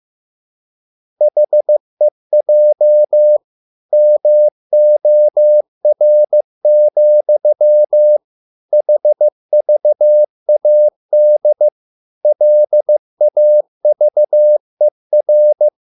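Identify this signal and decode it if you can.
Morse code, heard as a keyed tone, about 15 wpm, 600 Hz. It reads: HEJ MOR, HVAD LAVER
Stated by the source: Morse code at a slow 15 wpm from CW med Gnister.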